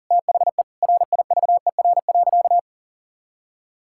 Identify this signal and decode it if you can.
Morse code, heard as a keyed tone, about 40 wpm, 700 Hz. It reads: THE RIVER.
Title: Sentences from Top 1000 Words - Review Part 6 - 40wpm